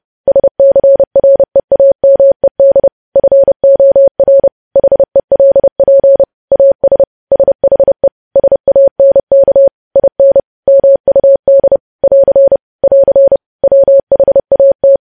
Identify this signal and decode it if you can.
Morse code, heard as a keyed tone, about 30 wpm, 550 Hz. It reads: SCREAMED FOR HELP AS SHE SANK IN MUD <AR> <AR> WHAT
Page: news headlines in morse code at 30 wpm